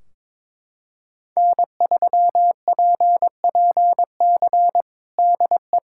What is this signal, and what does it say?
Morse code, heard as a keyed tone, about 22 wpm, 700 Hz. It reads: N3PPC DE
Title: Practice QSO - Exchange 4 - Ending - 22wpm